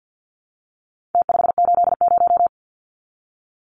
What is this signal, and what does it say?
Morse code, heard as a keyed tone, about 50 wpm, 700 Hz. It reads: T580
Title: Numbers - Contest Style - 4-Digit 50wpm